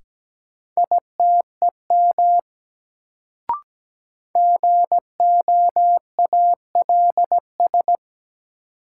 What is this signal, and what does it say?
Morse code, heard as a keyed tone, about 17 wpm, 700 Hz. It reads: ITEM GOALS
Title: Sentences from Top 1000 Words - Review Part 6 - New Word Drill - 17wpm